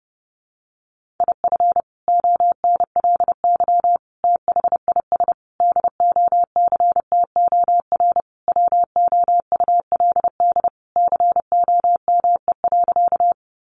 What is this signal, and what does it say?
Morse code, heard as a keyed tone, about 30 wpm, 700 Hz. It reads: IF ONLY THIS DOCTOR WOULD COME.
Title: Sentences from Top 1500 Words 30wpm